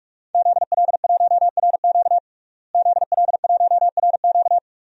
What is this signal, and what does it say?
Morse code, heard as a keyed tone, about 43 wpm, 700 Hz. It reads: ZL1RQ ZL1RQ
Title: DXCC-100 Call Signs - 5 Characters - Repeated 2x - 45wpm